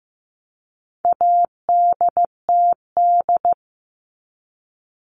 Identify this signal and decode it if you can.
Morse code, heard as a keyed tone, about 15 wpm, 700 Hz. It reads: ADTD
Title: All Cut Numbers - 4-Digit 15wpm